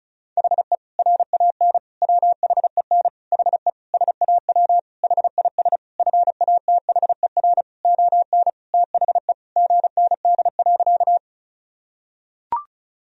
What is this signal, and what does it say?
Morse code, heard as a keyed tone, about 36 wpm, 700 Hz. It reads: HE RAN WHEN HE SAW HIS FATHER ON THE GND.
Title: Sentences from Top 500 Words and Abbreviations 35wpm